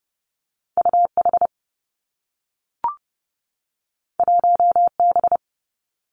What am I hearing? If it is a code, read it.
Morse code, heard as a keyed tone, about 30 wpm, 700 Hz. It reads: U H 1 B
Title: Two Challenging Letter-Number Combinations 30wpm